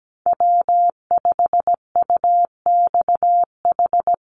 Morse code, heard as a keyed tone, about 17 wpm, 700 Hz. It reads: W5UXH
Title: Call Signs Repeated 2x - 17wpm